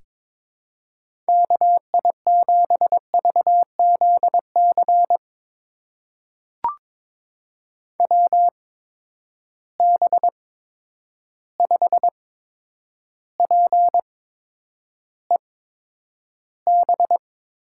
Morse code, heard as a keyed tone, about 22 wpm, spaced as for 9 wpm, 700 Hz. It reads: KI7VZC WB5PEB
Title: USA Call Signs - 6 Characters - 8x Character Spacing - 22wpm